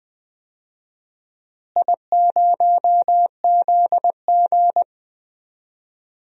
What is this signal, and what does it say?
Morse code, heard as a keyed tone, about 20 wpm, 700 Hz. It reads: I0ZG
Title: DXCC-100 Call Signs - 4 Characters - Repeated 3x - 20wpm